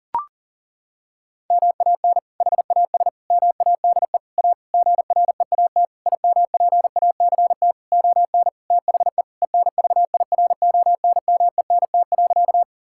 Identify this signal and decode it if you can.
Morse code, heard as a keyed tone, about 40 wpm, 700 Hz. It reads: MAN HAS MADE A GREAT IMPACT ON THE ENVIRONMENT.
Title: Sentences from Top 1000 Words - Review Part 6 - 40wpm